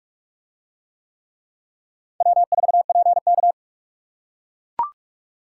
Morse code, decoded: W4PK

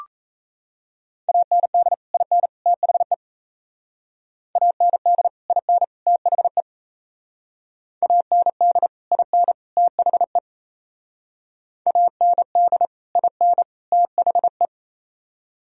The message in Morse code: AND IN THE AND IN THE AND IN THE AND IN THE

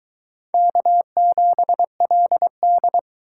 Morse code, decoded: K7LD

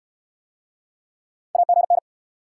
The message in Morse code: AGN

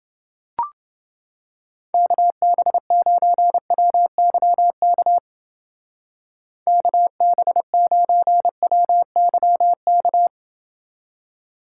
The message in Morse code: KB9WYK KB9WYK

